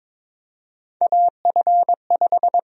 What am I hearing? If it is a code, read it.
Morse code, heard as a keyed tone, about 21 wpm, 700 Hz. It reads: AF5
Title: USA Call Sign Prefixes - 22wpm